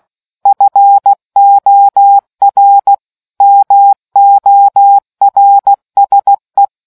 Morse code, heard as a keyed tone, about 16 wpm, 800 Hz. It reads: FOR MORSE